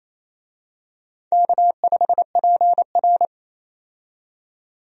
Morse code, heard as a keyed tone, about 28 wpm, 700 Hz. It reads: K5PR
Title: USA Call Signs - 4 Characters - 28wpm